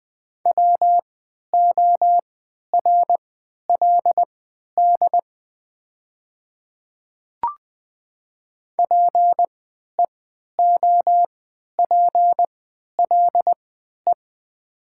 Morse code, heard as a keyed tone, about 20 wpm, spaced as for 11 wpm, 700 Hz. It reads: WORLD PEOPLE
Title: 3x Character Spacing - Top 200 Words - 20wpm